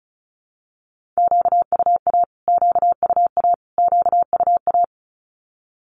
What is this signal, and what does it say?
Morse code, heard as a keyed tone, about 34 wpm, 700 Hz. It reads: QUA QUA QUA
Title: Q-Codes Repeated 3x 35wpm